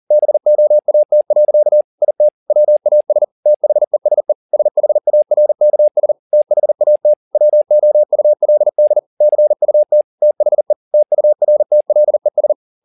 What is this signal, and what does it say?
Morse code, heard as a keyed tone, about 40 wpm, 600 Hz. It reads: BOAT. IT WAS THESE SHARKS THAT WOULD CUT THE TURTLES
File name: the old man and the sea 2 40wpm